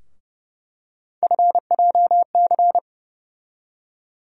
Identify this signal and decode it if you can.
Morse code, heard as a keyed tone, about 30 wpm, 700 Hz. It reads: FJC
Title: USA Call Sign Postfixes - 3 Letters 30wpm